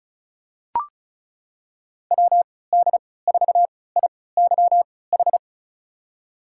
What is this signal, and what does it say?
Morse code, heard as a keyed tone, about 36 wpm, spaced as for 20 wpm, 700 Hz. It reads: WD4IYH
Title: USA Call Signs - 3x Character Spacing - 35wpm